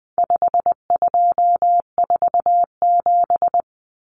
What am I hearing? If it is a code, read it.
Morse code, heard as a keyed tone, about 20 wpm, 700 Hz. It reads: 5247
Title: Common Cut Numbers - 4-Digit 20wpm